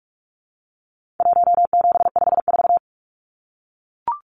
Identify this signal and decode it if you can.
Morse code, heard as a keyed tone, about 44 wpm, 700 Hz. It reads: 1754
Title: Numbers - Contest Style - 4-Digit 45wpm